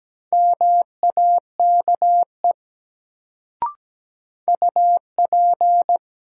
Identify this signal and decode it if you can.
Morse code, heard as a keyed tone, about 17 wpm, 700 Hz. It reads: MAKE UP
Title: Sentences from Top 100 Words - New Word Drill - 17wpm